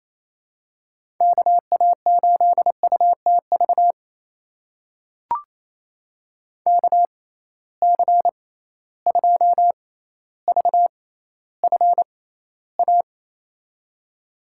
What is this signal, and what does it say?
Morse code, heard as a keyed tone, about 28 wpm, spaced as for 14 wpm, 700 Hz. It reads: KA8UTV KC2VFA
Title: USA Call Signs - 6 Characters - 6x Character Spacing - 28wpm